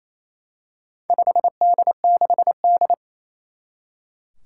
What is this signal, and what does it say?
Morse code, heard as a keyed tone, about 27 wpm, 700 Hz. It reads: 5D6D
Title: Four Challenging Letter-Number Combinations 28wpm